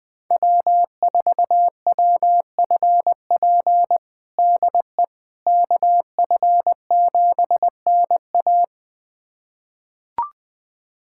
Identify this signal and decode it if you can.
Morse code, heard as a keyed tone, about 20 wpm, 700 Hz. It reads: W4WFP DE KF7NA
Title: USA Callsign DE Callsign - 5 Characters - 20wpm